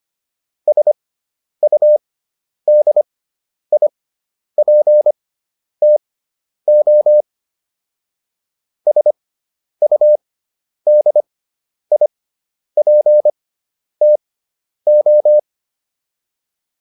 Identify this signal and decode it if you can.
Morse code, heard as a keyed tone, about 25 wpm, spaced as for 10 wpm, 600 Hz. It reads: SUDIPTO SUDIPTO
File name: QSO 410 10